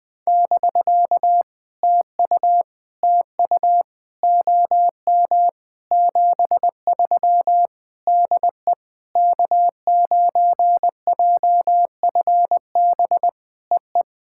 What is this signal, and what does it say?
Morse code, heard as a keyed tone, about 20 wpm, 700 Hz. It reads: <BK> TU TU OM 73 DE K9JFB EE
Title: POTA QSO Practice - 20wpm